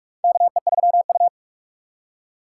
Morse code, heard as a keyed tone, about 46 wpm, 700 Hz. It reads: KE3U
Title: USA Call Signs - 3x Character Spacing - 45wpm